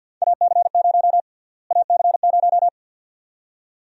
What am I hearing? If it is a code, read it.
Morse code, heard as a keyed tone, about 51 wpm, 700 Hz. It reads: AC0 AC0
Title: USA Call Sign Prefixes - Repeated 2x - 50wpm